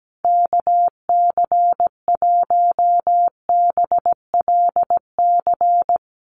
Morse code, heard as a keyed tone, about 17 wpm, 700 Hz. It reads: KC1BLC
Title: USA Call Signs Repeated 3x 17wpm